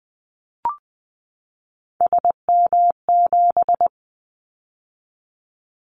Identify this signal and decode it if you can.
Morse code, heard as a keyed tone, about 20 wpm, 700 Hz. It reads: SM7
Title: Callsign Prefixes for Top 25 DXCC - 20wpm